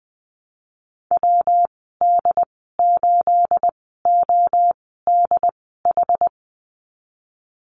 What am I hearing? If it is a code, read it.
Morse code, heard as a keyed tone, about 19 wpm, spaced as for 14 wpm, 700 Hz. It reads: WD8ODH
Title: 219 Lesson 35g - Number 0 - Call Signs 20wpm